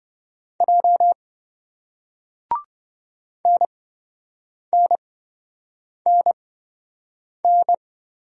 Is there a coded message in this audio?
Yes: Morse code from 0.6 to 7.7 s, heard as a keyed tone, about 28 wpm, spaced as for 7 wpm, 700 Hz. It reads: J NNNN